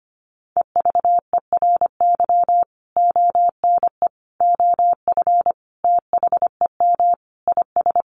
Morse code, heard as a keyed tone, about 25 wpm, 700 Hz. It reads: EVERY ONE OF THEM IS
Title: Sentences from Top 1100 Words 25wpm